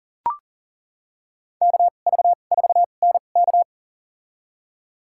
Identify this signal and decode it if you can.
Morse code, heard as a keyed tone, about 40 wpm, spaced as for 29 wpm, 700 Hz. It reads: KV4NK
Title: USA Call Signs - 5 Characters - 2x Character Spacing - 40wpm